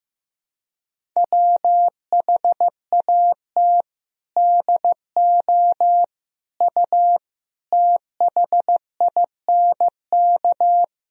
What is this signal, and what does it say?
Morse code, heard as a keyed tone, about 15 wpm, 700 Hz. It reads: WHAT DO U THINK
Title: Sentences from Top 300 Words and Abbreviations 15wpm